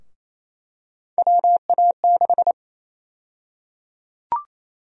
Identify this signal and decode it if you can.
Morse code, heard as a keyed tone, about 28 wpm, 700 Hz. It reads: WA6
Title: USA Call Sign Prefixes - 28wpm